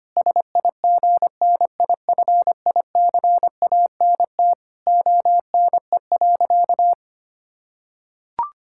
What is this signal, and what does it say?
Morse code, heard as a keyed tone, about 25 wpm, 700 Hz. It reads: SIGNIFICANT ONE.
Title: Sentences from Top 1000 Words - Review Part 6 - 25wpm